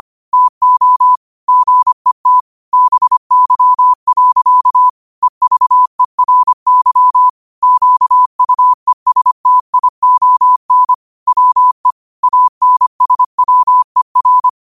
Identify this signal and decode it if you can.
Morse code, heard as a keyed tone, about 25 wpm, 1000 Hz. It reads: TO GET BY. EVERY QUESTION WE ANSWER